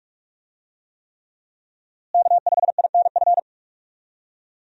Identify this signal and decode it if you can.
Morse code, heard as a keyed tone, about 46 wpm, 700 Hz. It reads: K5INF
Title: USA Call Signs - 5 Characters - Repeated 3x - 45wpm